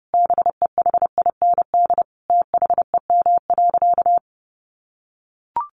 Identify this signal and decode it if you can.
Morse code, heard as a keyed tone, about 30 wpm, 700 Hz. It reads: BEHIND THEM.
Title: Sentences from Top 1400 Words 30wpm